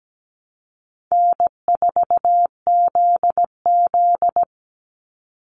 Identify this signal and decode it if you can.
Morse code, heard as a keyed tone, about 17 wpm, 700 Hz. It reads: N4ZZ
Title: Call Signs Repeated 3x - 17wpm